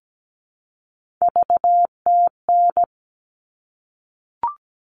Morse code, heard as a keyed tone, about 17 wpm, 700 Hz. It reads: VTN
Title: All Cut Numbers - 3-Digit 17wpm